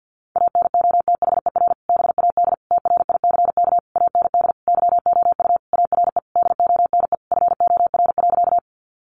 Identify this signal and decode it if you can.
Morse code, heard as a keyed tone, about 50 wpm, 700 Hz. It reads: ANOTHER BAD TRICK AND YOU ARE DONE FOR.
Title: Sentences from Top 1500 Words 50wpm